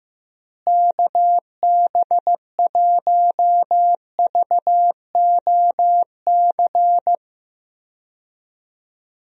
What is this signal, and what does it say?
Morse code, heard as a keyed tone, about 15 wpm, 700 Hz. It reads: KB1VOC
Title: USA Call Signs - 6 Characters - Repeated 2x - 15wpm